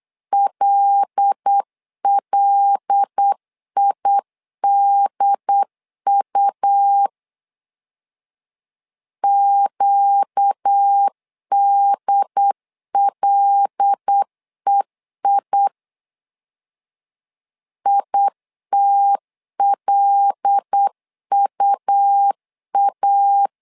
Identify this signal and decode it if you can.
Morse code, heard as a keyed tone, about 8 wpm, 800 Hz. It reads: LLIDU QDLEI ITLUA
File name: LEZIONI / 5 LBVC / 5 L 8wpm 2831